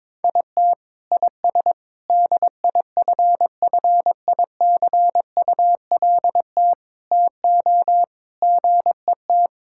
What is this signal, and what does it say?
Morse code, heard as a keyed tone, about 22 wpm, 700 Hz. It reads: IT IS DIFFICULT TO GET
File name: Sets of 9 Words - 22wpm